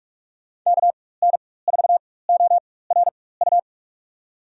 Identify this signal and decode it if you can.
Morse code, heard as a keyed tone, about 46 wpm, spaced as for 21 wpm, 700 Hz. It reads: KN4ORU
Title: USA Call Signs - 6 Characters - 4x Character Spacing - 45wpm